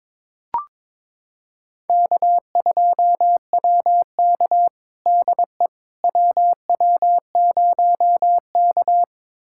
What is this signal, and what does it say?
Morse code, heard as a keyed tone, about 22 wpm, 700 Hz. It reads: K2WK DE WW0K
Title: USA Callsign DE Callsign - 4 Characters - 22wpm